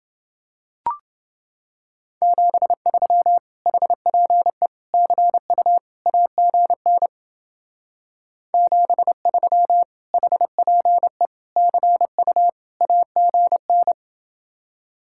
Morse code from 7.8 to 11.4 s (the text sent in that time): 73 HPE